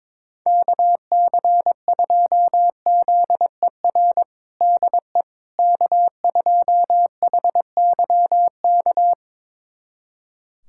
Morse code, heard as a keyed tone, about 22 wpm, 700 Hz. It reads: KC2ZER DE K2HYK
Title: USA Callsign DE Callsign 22wpm